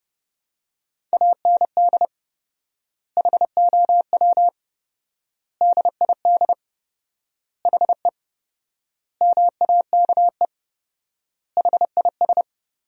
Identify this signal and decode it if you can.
Morse code, heard as a keyed tone, about 30 wpm, 700 Hz. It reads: AND HOW DID HE MAKE HIS
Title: 4x Word-Spacing - Sentences from Top 400 Words - 30wpm